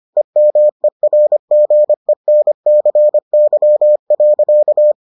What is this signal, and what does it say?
Morse code, heard as a keyed tone, about 25 wpm, 600 Hz. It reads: EMERGENCY.